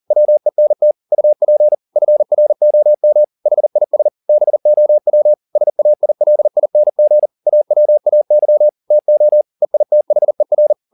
Morse code, heard as a keyed tone, about 40 wpm, 600 Hz. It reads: WENT UP FROM HIS BOW SAILING AWAY TO EITHER
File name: the old man and the sea 2 40wpm